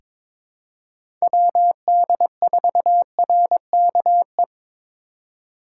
Morse code, heard as a keyed tone, about 22 wpm, 700 Hz. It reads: WD4RKE
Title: USA Call Signs - 6 Characters - 3x Character Spacing - 22wpm